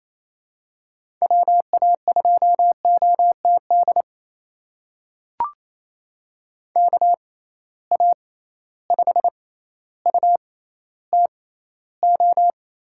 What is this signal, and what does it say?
Morse code, heard as a keyed tone, about 28 wpm, spaced as for 14 wpm, 700 Hz. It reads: WA2OTD KA5UTO
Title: USA Call Signs - 6 Characters - 6x Character Spacing - 28wpm